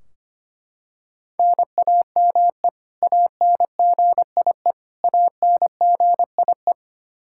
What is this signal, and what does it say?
Morse code, heard as a keyed tone, about 25 wpm, 700 Hz. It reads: NAME ANGIE ANGIE
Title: K1USN SST QSO Practice 25wpm